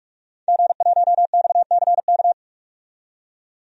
Morse code, heard as a keyed tone, about 46 wpm, 700 Hz. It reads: G1XCK
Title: DXCC-100 Call Signs - 5 Characters - 45wpm